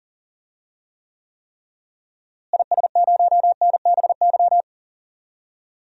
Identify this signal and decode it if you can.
Morse code, heard as a keyed tone, about 40 wpm, 700 Hz. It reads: IS0NBY
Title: DXCC-100 Call Signs - 6 Characters - Repeated 3x - 40wpm